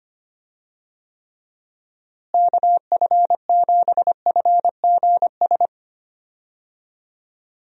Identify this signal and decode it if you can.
Morse code, heard as a keyed tone, about 25 wpm, 700 Hz. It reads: KF7FGS
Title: USA Call Signs - 6 Characters - Repeated 3x - 25wpm